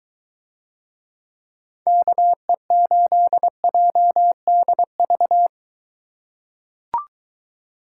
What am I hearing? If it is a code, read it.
Morse code, heard as a keyed tone, about 21 wpm, 700 Hz. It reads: KE8JDV